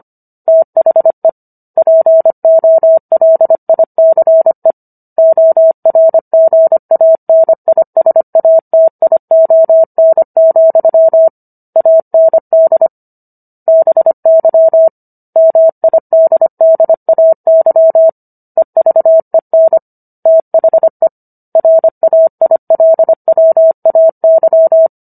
Morse code, heard as a keyed tone, about 25 wpm, 650 Hz. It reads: THE POLICE ORGANISATION, AND BY MIDDAY EVEN THE RAILWAY